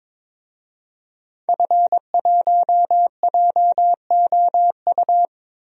Morse code, heard as a keyed tone, about 21 wpm, 700 Hz. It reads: F1JOU
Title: DXCC-100 Call Signs - 5 Characters - 22wpm